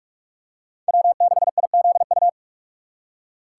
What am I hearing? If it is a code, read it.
Morse code, heard as a keyed tone, about 46 wpm, 700 Hz. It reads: W6IZU
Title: USA Call Signs - 6x Character Spacing - 45wpm